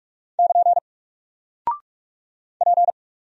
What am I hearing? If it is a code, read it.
Morse code, heard as a keyed tone, about 45 wpm, 700 Hz. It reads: <KN> P
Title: Single Character (Letter:Number:Punctuation:Prosign) Alt 45wpm